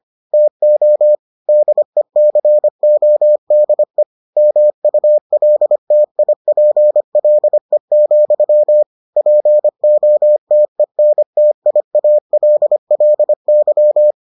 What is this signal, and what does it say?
Morse code, heard as a keyed tone, about 25 wpm, 600 Hz. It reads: TO DECODE MULTIPLE, POTENTIALLY